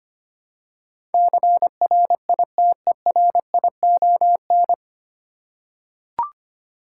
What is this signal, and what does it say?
Morse code, heard as a keyed tone, about 25 wpm, 700 Hz. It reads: CRITERION
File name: practice / 25wpm / 9 Letter Words - First 3 Letters Given - 25wpm